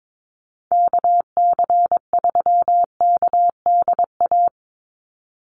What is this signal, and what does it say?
Morse code, heard as a keyed tone, about 22 wpm, 700 Hz. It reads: KC3KDA
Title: USA Call Signs 22wpm